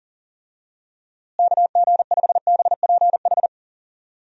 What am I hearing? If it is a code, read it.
Morse code, heard as a keyed tone, about 40 wpm, 700 Hz. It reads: KG5BPH
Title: USA Call Signs - 6 Characters - Repeated 3x - 40wpm